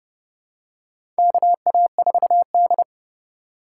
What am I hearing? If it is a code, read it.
Morse code, heard as a keyed tone, about 30 wpm, 700 Hz. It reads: KA4D
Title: USA Call Signs - 3x Character Spacing - 30wpm